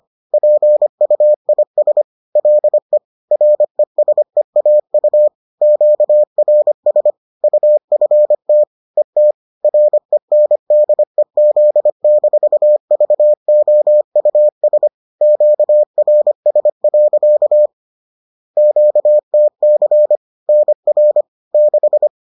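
Morse code, heard as a keyed tone, about 25 wpm, 600 Hz. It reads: PUIS LE RESEAU QRS UFT ET RENDEZ-VOUS QRS. QTC NR 6
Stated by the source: Bulletin sent in CW.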